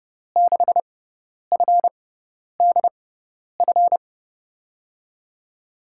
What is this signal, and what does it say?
Morse code, heard as a keyed tone, about 30 wpm, spaced as for 10 wpm, 700 Hz. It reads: 6FDF
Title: USA Call Signs - 6 Characters - 6x Character Spacing - 30wpm